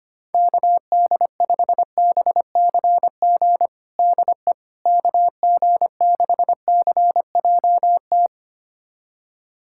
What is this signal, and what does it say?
Morse code, heard as a keyed tone, about 25 wpm, 700 Hz. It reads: KD5BCG DE KG6CJT
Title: USA Callsign DE Callsign 25wpm